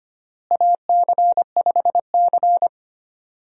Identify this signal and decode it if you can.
Morse code, heard as a keyed tone, about 25 wpm, 700 Hz. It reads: AC5C
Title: USA Call Signs - 4 Characters - Repeated 2x - 25wpm